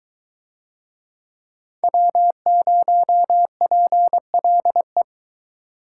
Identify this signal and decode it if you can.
Morse code, heard as a keyed tone, about 20 wpm, 700 Hz. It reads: W0PLE